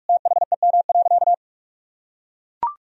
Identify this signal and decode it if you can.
Morse code, heard as a keyed tone, about 41 wpm, 700 Hz. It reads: THEM.
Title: Sentences from Top 300 Words and Abbreviations 45wpm